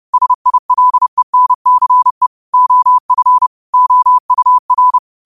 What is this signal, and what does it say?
Morse code, heard as a keyed tone, about 30 wpm, 1000 Hz. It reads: SILENCE OF OUR
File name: qotdfiles / 30wpm